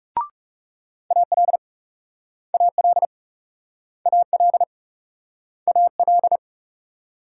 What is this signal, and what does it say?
Morse code, heard as a keyed tone, about 37 wpm, 700 Hz. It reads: AL AL AL AL
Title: Speed-Racing Multi-letter Phonemes - 30wpm